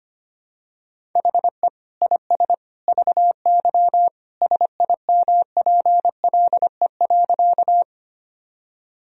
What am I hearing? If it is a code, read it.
Morse code, heard as a keyed tone, about 25 wpm, 700 Hz. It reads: HE IS VY SIMPLE.
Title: Sentences from Top 400 Words and Abbreviations 25wpm